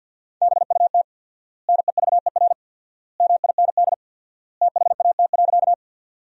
Morse code, heard as a keyed tone, about 50 wpm, 700 Hz. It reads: BUT NEVER MIND THAT.
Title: 4x Word-Spacing - Sentences from Top 500 Words - 50wpm